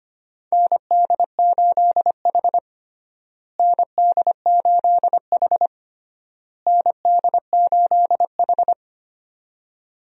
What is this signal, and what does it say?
Morse code, heard as a keyed tone, about 25 wpm, 700 Hz. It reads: ND8H ND8H ND8H
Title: USA Call Signs - 4 Characters - Repeated 3x - 25wpm